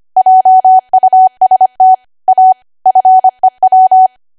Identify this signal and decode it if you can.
Morse code, heard as a keyed tone, about 25 wpm, 750 Hz. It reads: JUST A FEW
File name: cwa / W1AW-Level3 / 25WPM